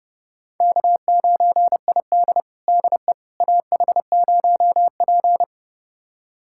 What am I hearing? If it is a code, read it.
Morse code, heard as a keyed tone, about 30 wpm, 700 Hz. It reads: K9ID DE AH0P
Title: USA Callsign DE Callsign - 4 Characters - 30wpm